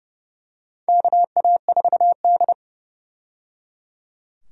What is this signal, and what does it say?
Morse code, heard as a keyed tone, about 30 wpm, 700 Hz. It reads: KA4D
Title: USA Call Signs - 4 Characters - 30wpm